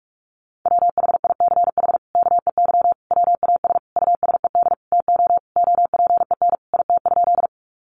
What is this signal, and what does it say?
Morse code, heard as a keyed tone, about 44 wpm, 700 Hz. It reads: WHICH KEY WAS USED TO OPEN IT?
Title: Sentences from Top 600 Words 45wpm